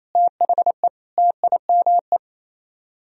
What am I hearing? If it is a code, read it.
Morse code, heard as a keyed tone, about 34 wpm, 700 Hz. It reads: THE TIME
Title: Speed-Racing - Sets of 3 Words 28wpm